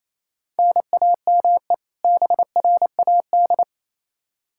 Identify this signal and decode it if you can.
Morse code, heard as a keyed tone, about 28 wpm, 700 Hz. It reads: NAME BRAD
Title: Practice QSO - Exchange 1 - Beginning - 28wpm